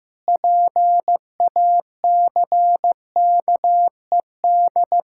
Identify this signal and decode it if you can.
Morse code, heard as a keyed tone, about 15 wpm, 700 Hz. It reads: PACKED
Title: Pangram Challenge v2 - 15wpm